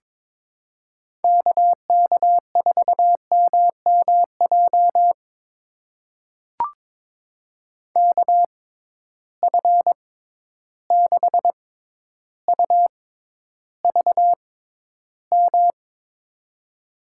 Morse code, heard as a keyed tone, about 22 wpm, spaced as for 11 wpm, 700 Hz. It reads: KK4MMJ KF6UVM
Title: USA Call Signs - 6 Characters - 6x Character Spacing - 22wpm